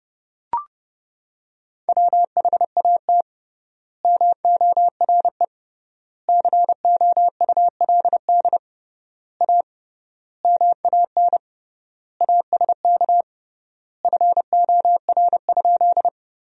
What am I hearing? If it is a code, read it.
Morse code, heard as a keyed tone, about 31 wpm, 700 Hz. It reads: WHAT MORE COULD A MAN ASK FOR?
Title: 3x Word-Spacing - Sentences from Top 200 Words - 30wpm